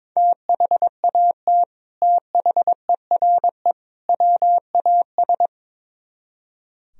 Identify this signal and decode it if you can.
Morse code, heard as a keyed tone, about 23 wpm, 700 Hz. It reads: THAT THERE WAS
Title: Speed-Racing - Sets of 3 Words 22wpm